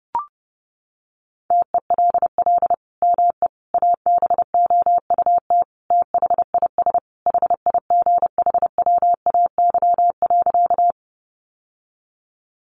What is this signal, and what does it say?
Morse code, heard as a keyed tone, about 30 wpm, 700 Hz. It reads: TELL ME ABOUT THIS HIGHWAY.
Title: Sentences from Top 1400 Words 30wpm